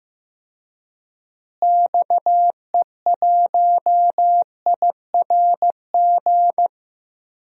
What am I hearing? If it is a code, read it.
Morse code, heard as a keyed tone, about 15 wpm, 700 Hz. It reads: XE1IRG
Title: DXCC-100 Call Signs - 6 Characters - Repeated 2x - 15wpm